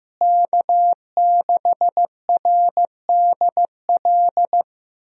Morse code, heard as a keyed tone, about 15 wpm, 700 Hz. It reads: K6RDL
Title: USA Call Signs - 5 Characters - Repeated 3x - 15wpm